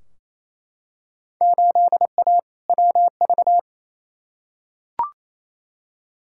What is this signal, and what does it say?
Morse code, heard as a keyed tone, about 28 wpm, 700 Hz. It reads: 8A WV
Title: Field Day Exchanges Encore - 28wpm